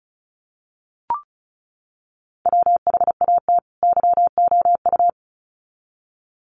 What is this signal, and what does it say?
Morse code, heard as a keyed tone, about 34 wpm, 700 Hz. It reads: WHAT YOU
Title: Sets of 2 Words 35wpm